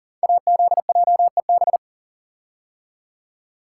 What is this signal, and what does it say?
Morse code, heard as a keyed tone, about 38 wpm, 700 Hz. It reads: AZJEB
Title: Code Groups - 5 Letter:Numbers - 40wpm